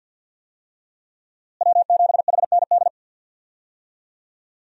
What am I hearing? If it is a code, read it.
Morse code, heard as a keyed tone, about 52 wpm, 700 Hz. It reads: W7HND
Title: USA Call Signs - 5 Characters - Repeated 2x - 50wpm